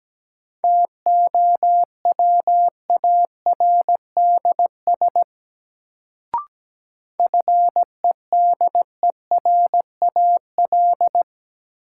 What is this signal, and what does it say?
Morse code, heard as a keyed tone, about 17 wpm, 700 Hz. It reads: TOWARDS FEDERAL
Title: Sentences from Top 1100 Words - New Word Drill Alt - 17wpm